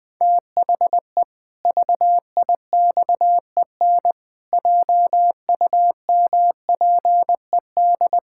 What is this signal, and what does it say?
Morse code, heard as a keyed tone, about 21 wpm, 700 Hz. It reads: THE VIXEN JUMPED
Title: Pangram Challenge v2 - 20wpm